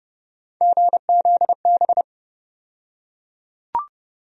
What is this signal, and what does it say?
Morse code, heard as a keyed tone, about 30 wpm, 700 Hz. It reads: GZB
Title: USA Call Sign Postfixes - 3 Letters 30wpm